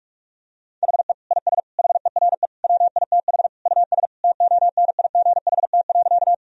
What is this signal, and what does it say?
Morse code, heard as a keyed tone, about 46 wpm, 700 Hz. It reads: HE IS HERE WITH US TONIGHT.
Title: Sentences from Top 1000 Words - Review Part 3 - 45wpm